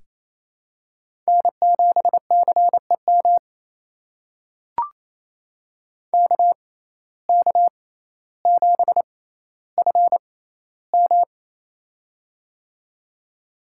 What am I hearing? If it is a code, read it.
Morse code, heard as a keyed tone, about 28 wpm, spaced as for 14 wpm, 700 Hz. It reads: N7CEM KK7FM
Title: USA Call Signs - 5 Characters - 6x Character Spacing - 28wpm